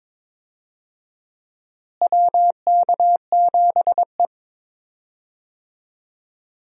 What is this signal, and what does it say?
Morse code, heard as a keyed tone, about 22 wpm, 700 Hz. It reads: WK7E
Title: USA Call Signs - 4 Characters - Repeated 3x - 22wpm